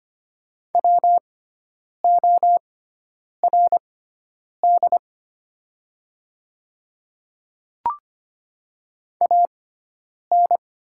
Morse code, heard as a keyed tone, about 25 wpm, spaced as for 9 wpm, 690 Hz. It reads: WORD AN
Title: 6x Character Spacing - Top 100 Words - 25wpm